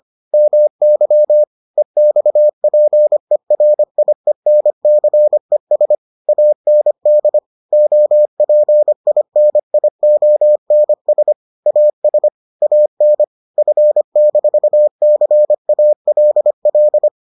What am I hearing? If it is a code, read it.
Morse code, heard as a keyed tone, about 25 wpm, 600 Hz. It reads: MY EXPERIENCES AND OPINIONS AS AN F-CALL